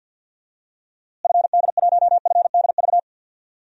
Morse code, heard as a keyed tone, about 51 wpm, 700 Hz. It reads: FD1FDV
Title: DXCC-100 Call Signs - 6 Characters - Repeated 3x - 50wpm